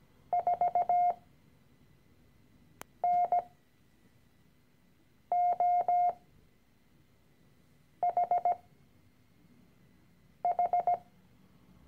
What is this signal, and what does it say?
Morse code, heard as a keyed tone, about 17 wpm, spaced as for 4 wpm, 710 Hz. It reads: 4NOHH
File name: CW-Online-Kurs / Anfänger / Lektion 013